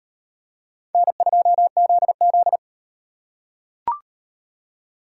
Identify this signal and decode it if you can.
Morse code, heard as a keyed tone, about 33 wpm, 700 Hz. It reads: N2ZZ